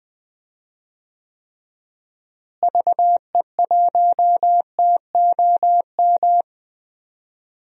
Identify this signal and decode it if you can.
Morse code, heard as a keyed tone, about 20 wpm, 700 Hz. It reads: VE1TOM
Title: DXCC-100 Call Signs - 6 Characters - Repeated 3x - 20wpm